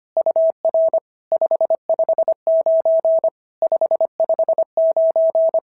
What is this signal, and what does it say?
Morse code, heard as a keyed tone, about 24 wpm, 650 Hz. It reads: UR 559 559
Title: SOTA QSO Practice 25wpm